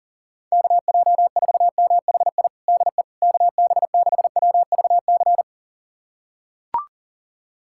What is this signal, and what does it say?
Morse code, heard as a keyed tone, about 40 wpm, 700 Hz. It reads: KJ4MHI DE KB6WVC
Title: USA Callsign DE Callsign - 6 Characters - 40wpm